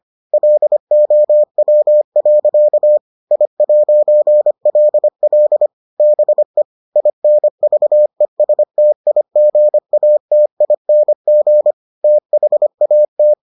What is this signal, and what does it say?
Morse code, heard as a keyed tone, about 25 wpm, 600 Hz. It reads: LOW. I'LL BE INVESTIGATING THAT